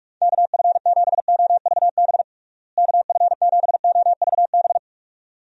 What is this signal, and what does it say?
Morse code, heard as a keyed tone, about 43 wpm, 700 Hz. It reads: KF7OVB KF7OVB
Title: USA Call Signs - 6 Characters - Repeated 2x - 45wpm